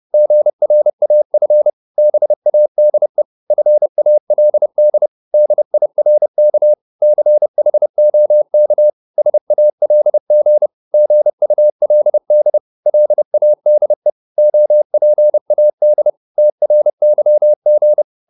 Morse code, heard as a keyed tone, about 30 wpm, 600 Hz. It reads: GRAF BADE FALD DIRK CHOK SALG GULD LADE OPAD TRYG
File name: Korte ord DK 30wpm